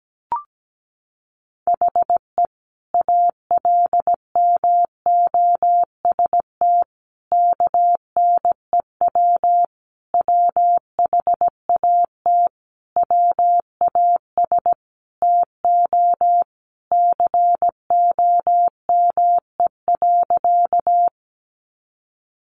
Morse code, heard as a keyed tone, about 17 wpm, 700 Hz. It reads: HE ALMOST KNEW WHAT WAS TO COME.
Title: Sentences from Top 700 Words 17wpm